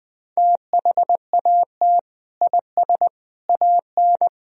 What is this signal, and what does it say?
Morse code, heard as a keyed tone, about 20 wpm, 700 Hz. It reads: THAT IS AN
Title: Sentences from Top 1000 Words - Review Part 1 - 20wpm